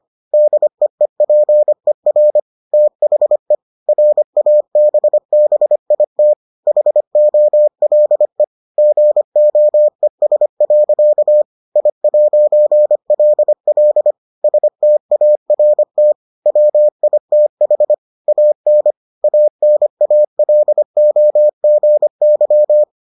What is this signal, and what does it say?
Morse code, heard as a keyed tone, about 25 wpm, 600 Hz. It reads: DEEPER THE RABBIT HOLE GOES. I'LL START WITH AN ANALOGY